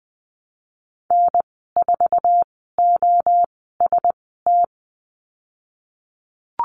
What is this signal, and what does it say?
Morse code, heard as a keyed tone, about 20 wpm, 700 Hz. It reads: N 4 O S T
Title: 048 Lesson 9g - Number 4 - Call Signs 20wpm